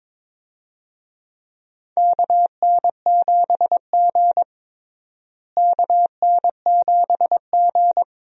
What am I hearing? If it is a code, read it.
Morse code, heard as a keyed tone, about 21 wpm, 700 Hz. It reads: KN7G KN7G
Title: USA Call Signs - 4 Characters - Repeated 2x - 22wpm